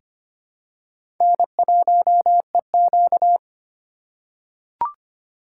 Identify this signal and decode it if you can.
Morse code, heard as a keyed tone, about 25 wpm, 700 Hz. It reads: N1EQ
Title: USA Call Signs - 4 Characters - 2x Character Spacing - 25wpm